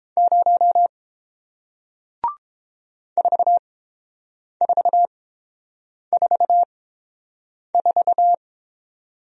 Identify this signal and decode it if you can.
Morse code, heard as a keyed tone, about 30 wpm, spaced as for 7 wpm, 700 Hz. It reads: 0 4444